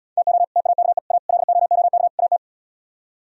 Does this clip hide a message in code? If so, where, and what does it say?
Yes: Morse code from 0.2 to 2.4 s, heard as a keyed tone, about 78 wpm, 700 Hz. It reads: TO TAKE A LOOK AT